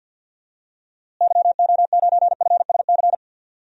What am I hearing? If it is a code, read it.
Morse code, heard as a keyed tone, about 51 wpm, 700 Hz. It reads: YO9FSC
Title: DXCC-100 Call Signs - 6 Characters - Repeated 2x - 50wpm